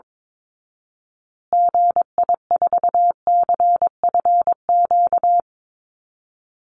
Morse code, heard as a keyed tone, about 22 wpm, 700 Hz. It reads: GI4CFQ
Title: Call Signs Repeated 3x - 22wpm